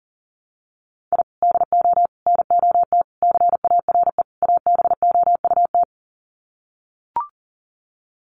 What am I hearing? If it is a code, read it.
Morse code, heard as a keyed tone, about 37 wpm, 700 Hz. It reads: I DO NOT CARE ABOUT
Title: Sets of 5 Words Encore2 40wpm